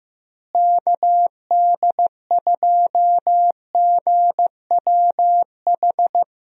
Morse code, heard as a keyed tone, about 15 wpm, 700 Hz. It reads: KD2GWH
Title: USA Call Signs - 6 Characters - Repeated 3x - 15wpm